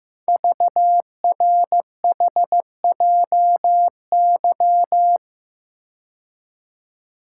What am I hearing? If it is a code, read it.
Morse code, heard as a keyed tone, about 15 wpm, 700 Hz. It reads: VRHJY
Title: Code Groups - 5 Letter:Numbers - 15wpm